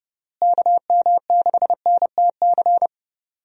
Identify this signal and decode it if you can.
Morse code, heard as a keyed tone, about 31 wpm, 700 Hz. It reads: KM6NTC
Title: USA Call Signs - 6 Characters - Repeated 3x - 30wpm